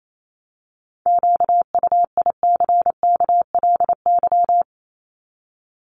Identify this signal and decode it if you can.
Morse code, heard as a keyed tone, about 28 wpm, 700 Hz. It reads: QUICKLY
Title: LY Suffix - Top 100 Words with Base Words 28wpm